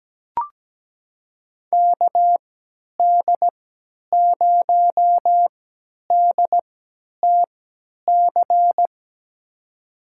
Morse code, heard as a keyed tone, about 17 wpm, spaced as for 10 wpm, 700 Hz. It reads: KD0DTC
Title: USA Call Signs - 6 Characters - 3x Character Spacing - 17wpm